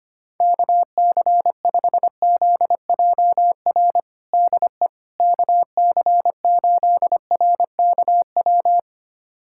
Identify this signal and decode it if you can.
Morse code, heard as a keyed tone, about 25 wpm, 700 Hz. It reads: KC5ZJR DE KC8RKW
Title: USA Callsign DE Callsign - 6 Characters - 25wpm